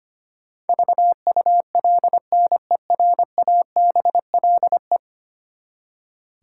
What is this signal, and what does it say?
Morse code, heard as a keyed tone, about 25 wpm, 700 Hz. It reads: VULNERABLE
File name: Ten Letter Words - 25wpm